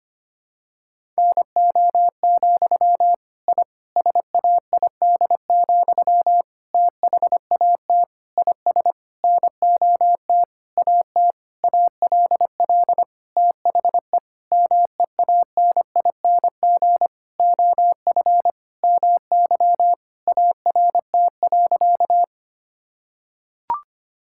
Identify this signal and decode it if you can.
Morse code, heard as a keyed tone, about 25 wpm, 700 Hz. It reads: NO, I SAID, THAT IS NOT AT ALL THE MEANING OF MY ART.
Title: Sentences from Top 1000 Words - Review Part 6 - 25wpm